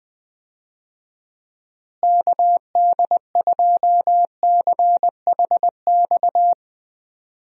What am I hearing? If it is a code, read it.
Morse code, heard as a keyed tone, about 20 wpm, 700 Hz. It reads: KD2CHX
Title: USA Call Signs - 6 Characters - Repeated 2x - 20wpm